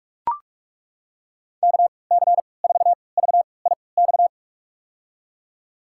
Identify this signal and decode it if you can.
Morse code, heard as a keyed tone, about 43 wpm, spaced as for 25 wpm, 700 Hz. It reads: KC4VIX
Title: USA Call Signs - 3x Character Spacing - 45wpm